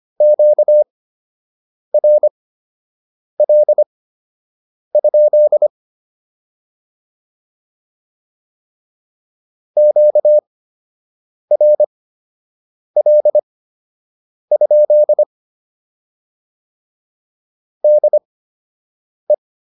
Morse code, heard as a keyed tone, about 25 wpm, spaced as for 7 wpm, 600 Hz. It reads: QRL? QRL? DE
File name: QSO 103 07